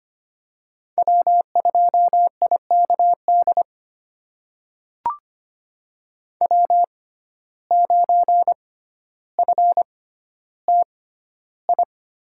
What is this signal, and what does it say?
Morse code, heard as a keyed tone, about 25 wpm, spaced as for 13 wpm, 700 Hz. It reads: W2IKD W9FTI
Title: USA Call Signs - 5 Characters - 6x Character Spacing - 25wpm